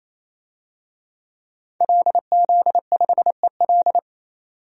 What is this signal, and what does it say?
Morse code, heard as a keyed tone, about 28 wpm, 700 Hz. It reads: LZ5EL
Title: DXCC-100 Call Signs - 5 Characters - 28wpm